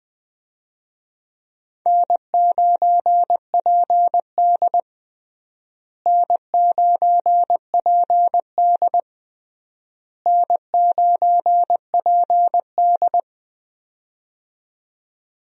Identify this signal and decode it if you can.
Morse code, heard as a keyed tone, about 21 wpm, 700 Hz. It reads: N9PD N9PD N9PD
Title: USA Call Signs - 4 Characters - Repeated 3x - 20wpm